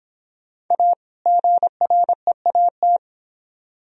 Morse code, heard as a keyed tone, about 32 wpm, 700 Hz. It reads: A GREAT